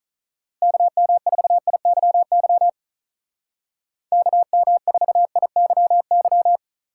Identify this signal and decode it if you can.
Morse code, heard as a keyed tone, about 45 wpm, 700 Hz. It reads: KM4IYY KM4IYY